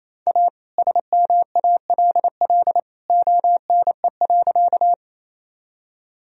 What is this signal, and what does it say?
Morse code, heard as a keyed tone, about 28 wpm, 700 Hz. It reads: A SMALL ONE.
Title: Sentences from Top 1000 Words - Review Part 1 - 28wpm